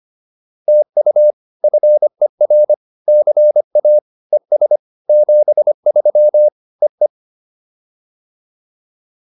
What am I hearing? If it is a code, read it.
Morse code, heard as a keyed tone, about 25 wpm, 590 Hz. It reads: TU FER CA ES 73 EE